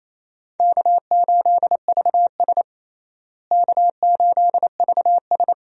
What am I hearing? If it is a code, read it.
Morse code, heard as a keyed tone, about 27 wpm, 700 Hz. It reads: K8VS K8VS
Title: USA Call Signs - 4 Characters - Repeated 3x - 28wpm